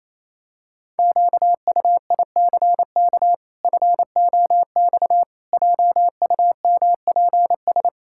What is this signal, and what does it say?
Morse code, heard as a keyed tone, about 28 wpm, 700 Hz. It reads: QUICK FOX JUMPS
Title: Pangram Challenge v2 - 28wpm